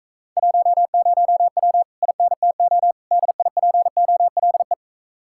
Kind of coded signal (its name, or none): Morse code